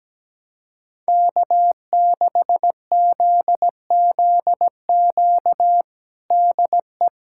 Morse code, heard as a keyed tone, about 17 wpm, 700 Hz. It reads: K6ZZQ DE
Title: USA Callsign DE Callsign 17wpm